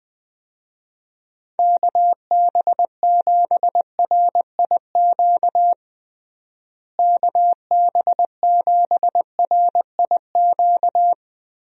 Morse code, heard as a keyed tone, about 20 wpm, 700 Hz. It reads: KB7RIQ KB7RIQ
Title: USA Call Signs - 6 Characters - Repeated 3x - 20wpm